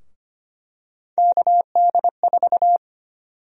Morse code, heard as a keyed tone, about 25 wpm, 700 Hz. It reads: KD4
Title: USA Call Sign Prefixes - 25wpm